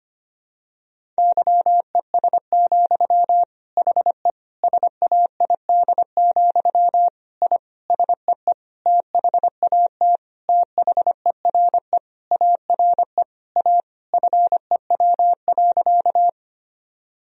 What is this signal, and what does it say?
Morse code, heard as a keyed tone, about 25 wpm, 700 Hz. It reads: YES, HE SAID, I SEE THAT THERE ARE A FEW.
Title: Sentences from Top 1000 Words - Review Part 1 - 25wpm